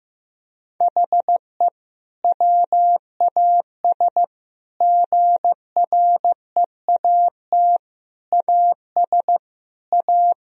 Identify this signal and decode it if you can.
Morse code, heard as a keyed tone, about 15 wpm, 700 Hz. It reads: HE WAS GREAT AS A
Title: Sentences from Top 1000 Words - Review Part 3 - 15wpm